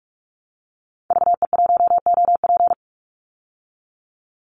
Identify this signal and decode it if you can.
Morse code, heard as a keyed tone, about 44 wpm, 700 Hz. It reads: VE1OP
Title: Call Signs Repeated 3x - 45wpm